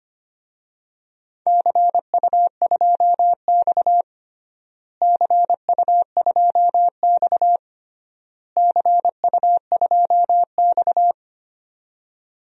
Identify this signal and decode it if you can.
Morse code, heard as a keyed tone, about 25 wpm, 700 Hz. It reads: CU2X CU2X CU2X
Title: DXCC-100 Call Signs - 4 Characters - Repeated 3x - 25wpm